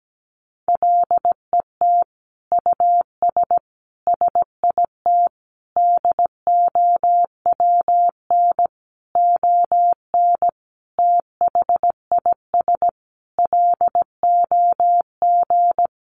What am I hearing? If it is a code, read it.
Morse code, heard as a keyed tone, about 17 wpm, 700 Hz. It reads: LET US SIT DOWN ON THIS LOG
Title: Sentences from Top 1400 Words 17wpm